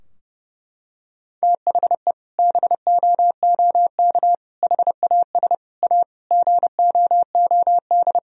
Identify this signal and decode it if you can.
Morse code, heard as a keyed tone, about 30 wpm, 700 Hz. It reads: THE BOOK HAS A GOOD
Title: Sentences from Top 1000 Words - Review Part 3 - 30wpm